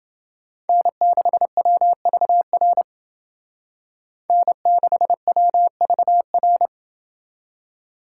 Morse code, heard as a keyed tone, about 28 wpm, 700 Hz. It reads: N6WVR N6WVR